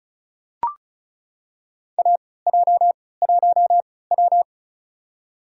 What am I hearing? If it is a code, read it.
Morse code, heard as a keyed tone, about 36 wpm, spaced as for 20 wpm, 700 Hz. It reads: AJ1W
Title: USA Call Signs - 4 Characters - 3x Character Spacing - 35wpm